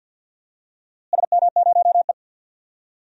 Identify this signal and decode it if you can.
Morse code, heard as a keyed tone, about 46 wpm, 700 Hz. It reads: SM0E
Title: DXCC-100 Call Signs - 4 Characters - Repeated 2x - 50wpm